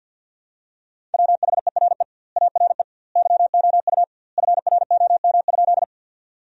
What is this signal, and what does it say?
Morse code, heard as a keyed tone, about 50 wpm, 700 Hz. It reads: WHERE ARE YOU FROM?
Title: 2x Word-Spacing - Sentences from Top 200 Words - 50wpm